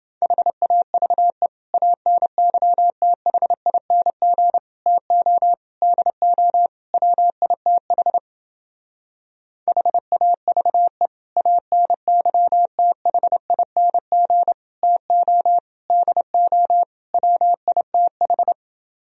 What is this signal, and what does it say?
Morse code, heard as a keyed tone, about 30 wpm, 700 Hz. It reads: HAVE ANYTHING TO DO WITH HAVE ANYTHING TO DO WITH